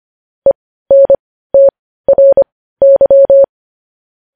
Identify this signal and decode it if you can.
Morse code, heard as a keyed tone, about 25 wpm, spaced as for 15 wpm, 550 Hz. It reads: ENTRY
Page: news-headlines-in-morse-code-at-15-wpm